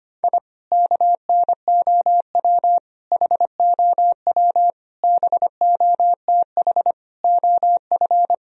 Morse code, heard as a keyed tone, about 25 wpm, 700 Hz. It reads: I KNOW HOW BOTH OF
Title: Sentences from Top 1000 Words - Review Part 1 - 25wpm